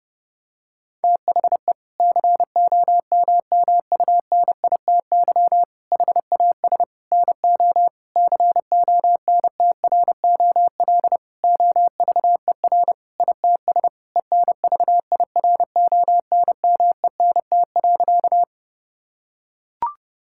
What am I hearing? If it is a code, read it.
Morse code, heard as a keyed tone, about 32 wpm, 700 Hz. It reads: THE COMMUNITY HAS NO CONTROL OVER ITS ENVIRONMENT.
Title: Sentences from Top 1000 Words - Review Part 5 - 30wpm